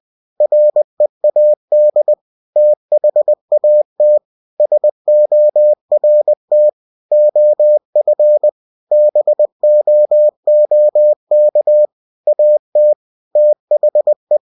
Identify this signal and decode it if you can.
Morse code, heard as a keyed tone, about 20 wpm, 600 Hz. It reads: READ THAT SORT OF BOOK AT THE
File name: cw-books / winnie-the-pooh / winnie-20wpm-page-0019